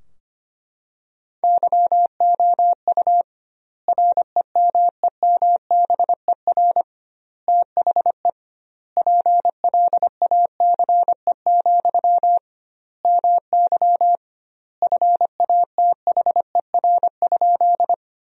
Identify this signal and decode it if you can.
Morse code, heard as a keyed tone, about 25 wpm, 700 Hz. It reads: YOU REMEMBER THE PLACE, MY FATHER?
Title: 2x Word-Spacing - Sentences from Top 400 Words - 25wpm